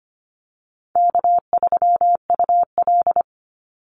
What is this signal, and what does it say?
Morse code, heard as a keyed tone, about 25 wpm, 700 Hz. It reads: K3UL
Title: Easy Call Signs Repeated 2x - 25wpm